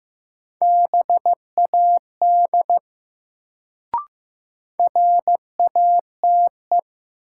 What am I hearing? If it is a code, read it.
Morse code, heard as a keyed tone, about 15 wpm, 700 Hz. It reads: BAD RATE
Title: Sentences from Top 700 Words - New Word Drill - 15wpm